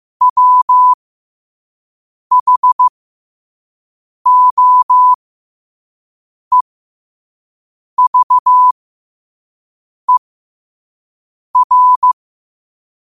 Morse code, heard as a keyed tone, about 15 wpm, spaced as for 5 wpm, 1000 Hz. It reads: WHOEVER
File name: qotdfiles / 5wpm